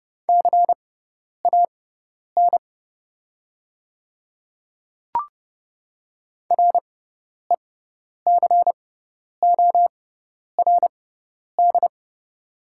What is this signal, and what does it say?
Morse code, heard as a keyed tone, about 29 wpm, spaced as for 10 wpm, 690 Hz. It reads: CAN RECORD
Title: 6x Character Spacing - Top 500 Words - 30wpm